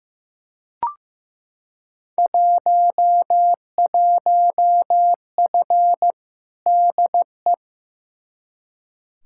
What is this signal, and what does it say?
Morse code, heard as a keyed tone, about 15 wpm, 700 Hz. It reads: 11F DE
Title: Field Day Exchanges Encore - 15wpm